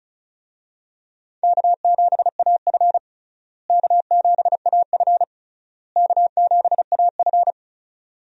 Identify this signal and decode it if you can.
Morse code, heard as a keyed tone, about 34 wpm, 700 Hz. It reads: K7AF K7AF K7AF
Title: USA Call Signs - 4 Characters - Repeated 3x - 35wpm